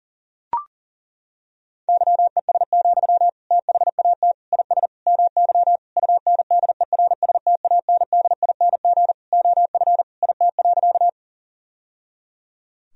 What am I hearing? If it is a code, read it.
Morse code, heard as a keyed tone, about 40 wpm, 700 Hz. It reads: YES, THAT IS MY UNDERSTANDING OF IT.
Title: Sentences from Top 1000 Words - Review Part 3 - 40wpm